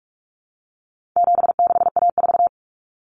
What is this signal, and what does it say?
Morse code, heard as a keyed tone, about 44 wpm, 700 Hz. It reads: 76A4
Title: Common Cut Numbers - 4-Digit 45wpm